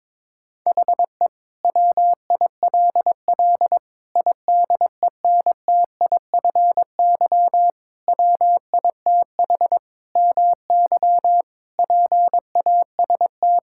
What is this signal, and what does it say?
Morse code, heard as a keyed tone, about 22 wpm, 700 Hz. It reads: HE WILL IDENTIFY WITH MY PAST
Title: Sentences from Top 1000 Words - Review Part 6 - 22wpm